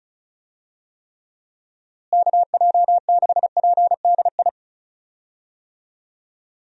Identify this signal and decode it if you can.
Morse code, heard as a keyed tone, about 36 wpm, 700 Hz. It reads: KJ6PDI
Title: USA Call Signs - 6 Characters - Repeated 3x - 35wpm